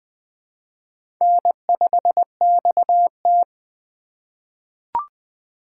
Morse code, heard as a keyed tone, about 20 wpm, 700 Hz. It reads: N5XT
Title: USA Call Signs - 4 Characters - 2x Character Spacing - 20wpm